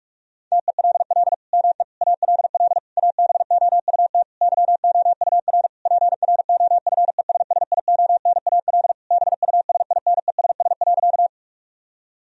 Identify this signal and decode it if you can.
Morse code, heard as a keyed tone, about 41 wpm, 700 Hz. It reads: TELL ME ALL ABOUT YOUR PROFESSIONAL BUSINESS.
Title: Sentences from Top 1000 Words - Review Part 4 - 45wpm